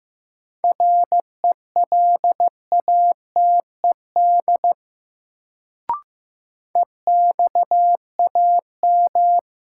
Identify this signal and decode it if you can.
Morse code, heard as a keyed tone, about 15 wpm, 700 Hz. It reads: RELATED EXAM
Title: Sentences from Top 1000 Words - Review Part 5 - New Word Drill - 15wpm